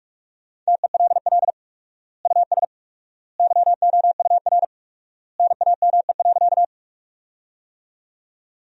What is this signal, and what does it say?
Morse code, heard as a keyed tone, about 47 wpm, 700 Hz. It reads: TELL US YOUR NAME.
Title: 4x Word-Spacing - Sentences from Top 200 Words - 45wpm